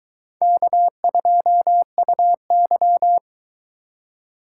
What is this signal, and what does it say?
Morse code, heard as a keyed tone, about 23 wpm, 700 Hz. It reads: K2UY